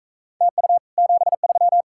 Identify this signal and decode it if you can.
Morse code, heard as a keyed tone, about 45 wpm, 700 Hz. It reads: TU 73